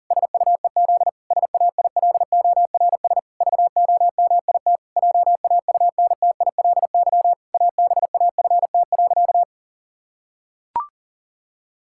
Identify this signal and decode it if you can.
Morse code, heard as a keyed tone, about 40 wpm, 700 Hz. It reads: SUEZ SAILORS VOMIT JAUNTILY ABAFT.
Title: Pangram Challenge v2 - 40wpm